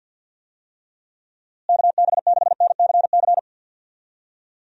Morse code, heard as a keyed tone, about 51 wpm, 700 Hz. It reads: KB6NCC
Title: USA Call Signs - 6 Characters - Repeated 2x - 50wpm